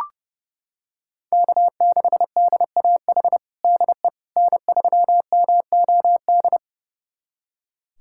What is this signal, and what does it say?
Morse code, heard as a keyed tone, about 30 wpm, 700 Hz. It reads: K6DAH DE N3MOD
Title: USA Callsign DE Callsign - 5 Characters - 30wpm